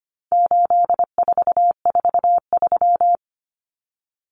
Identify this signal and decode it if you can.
Morse code, heard as a keyed tone, about 25 wpm, 700 Hz. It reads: 8443
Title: Common Cut Numbers - 4-Digit 25wpm